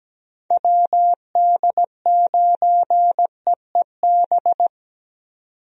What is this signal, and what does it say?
Morse code, heard as a keyed tone, about 17 wpm, 700 Hz. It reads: WD9EEB
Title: USA Call Signs - 6 Characters - 17wpm